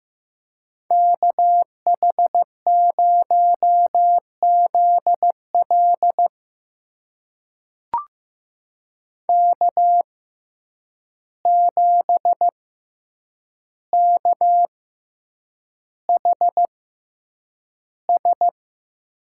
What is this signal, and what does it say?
Morse code, heard as a keyed tone, about 15 wpm, spaced as for 8 wpm, 700 Hz. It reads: KH0ZL K7KHS
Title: USA Call Signs - 5 Characters - 6x Character Spacing - 15wpm